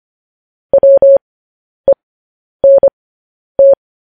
Morse code, heard as a keyed tone, about 25 wpm, spaced as for 10 wpm, 550 Hz. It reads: WENT
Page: News Headlines in Morse Code at 10 WPM